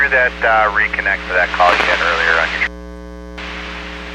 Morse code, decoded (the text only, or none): none